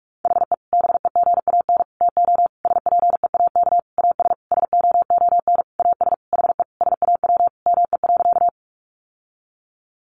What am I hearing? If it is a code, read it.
Morse code, heard as a keyed tone, about 44 wpm, 700 Hz. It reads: HE BEGAN TO SPEAK AS SOON AS HE SAW ME.
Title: Sentences from Top 1100 Words 45wpm